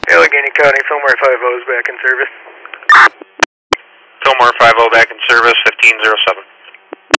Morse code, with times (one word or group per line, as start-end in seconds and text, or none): none